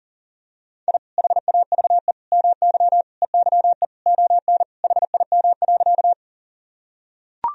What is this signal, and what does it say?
Morse code, heard as a keyed tone, about 40 wpm, 700 Hz. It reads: I HAVE MY EYE ON HIM.
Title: Sentences from Top 1000 Words - Review Part 1 - 40wpm